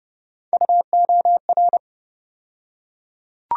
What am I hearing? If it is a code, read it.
Morse code, heard as a keyed tone, about 31 wpm, 700 Hz. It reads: UOR
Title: USA Call Sign Postfixes - 3 Letters 30wpm